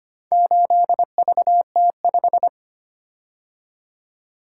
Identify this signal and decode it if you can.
Morse code, heard as a keyed tone, about 25 wpm, 700 Hz. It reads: 8VT5
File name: practice / 25wpm / Four Letter-Number Combinations - 25wpm